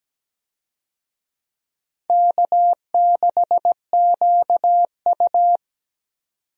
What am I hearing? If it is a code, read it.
Morse code, heard as a keyed tone, about 17 wpm, 700 Hz. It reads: K6QU
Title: USA Call Signs - 4 Characters - 17wpm